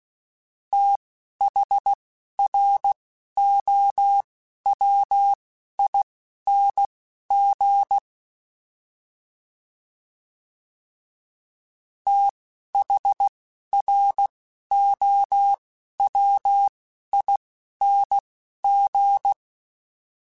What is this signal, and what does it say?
Morse code, heard as a keyed tone, about 16 wpm, spaced as for 11 wpm, 800 Hz. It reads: THROWING THROWING